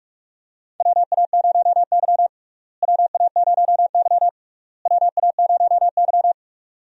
Morse code, heard as a keyed tone, about 43 wpm, 700 Hz. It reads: WA0Y WA0Y WA0Y
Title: USA Call Signs - 4 Characters - Repeated 3x - 45wpm